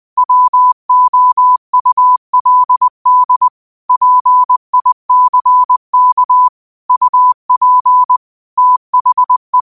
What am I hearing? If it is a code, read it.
Morse code, heard as a keyed tone, about 20 wpm, 1000 Hz. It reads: WOULD PICK UP THE